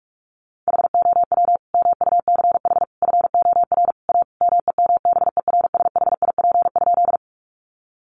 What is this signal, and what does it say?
Morse code, heard as a keyed tone, about 44 wpm, 700 Hz. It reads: HOW MUCH FOR A MEMBERSHIP?
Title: Sentences from Top 1400 Words 45wpm